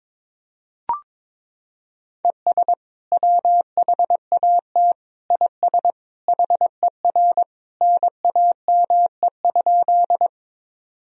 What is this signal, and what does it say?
Morse code, heard as a keyed tone, about 22 wpm, 700 Hz. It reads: ES WHAT IS HER NAME?
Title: Sentences from Top 200 Words and Abbreviations 22wpm